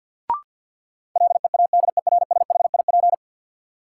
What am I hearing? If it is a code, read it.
Morse code, heard as a keyed tone, about 52 wpm, 700 Hz. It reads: LEADERSHIP
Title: Sentences from Top 1600 Words - New Word Drill Alt - 50wpm